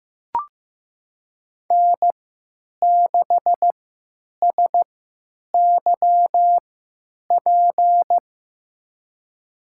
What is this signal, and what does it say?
Morse code, heard as a keyed tone, about 15 wpm, spaced as for 9 wpm, 700 Hz. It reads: N6SYP
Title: USA Call Signs - 5 Characters - 3x Character Spacing - 15wpm